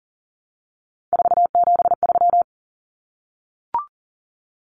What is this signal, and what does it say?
Morse code, heard as a keyed tone, about 40 wpm, 700 Hz. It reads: 473
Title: Numbers - 3-Digit 40wpm